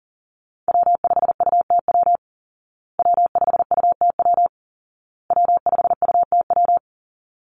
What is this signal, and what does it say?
Morse code, heard as a keyed tone, about 40 wpm, 700 Hz. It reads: W5UTW W5UTW W5UTW
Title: Call Signs Repeated 2x - 40wpm